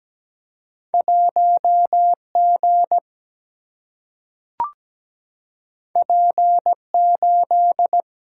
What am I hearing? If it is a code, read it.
Morse code, heard as a keyed tone, about 17 wpm, 700 Hz. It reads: 1G P8
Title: Single-Letter Single-Number 17wpm